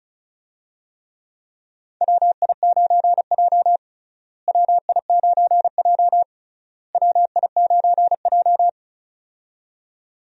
Morse code, heard as a keyed tone, about 34 wpm, 700 Hz. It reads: WI9J WI9J WI9J
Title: USA Call Signs - 4 Characters - Repeated 3x - 35wpm